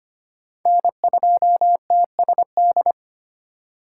Morse code, heard as a keyed tone, about 32 wpm, 700 Hz. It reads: N2TSD